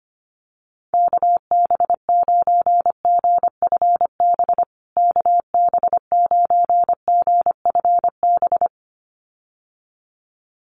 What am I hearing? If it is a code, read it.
Morse code, heard as a keyed tone, about 25 wpm, 700 Hz. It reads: KB9GFB KB9GFB
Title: USA Call Signs Repeated 2x 25wpm